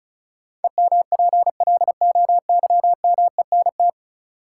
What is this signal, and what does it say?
Morse code, heard as a keyed tone, about 35 wpm, 700 Hz. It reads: EMPLOYMENT
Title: Sentences from Top 1500 Words - New Word Drill Alt - 35wpm